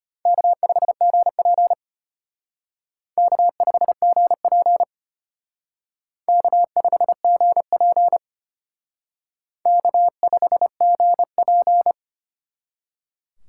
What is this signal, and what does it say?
Morse code, heard as a keyed tone, about 32 wpm, 700 Hz. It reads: K5GP K5GP K5GP K5GP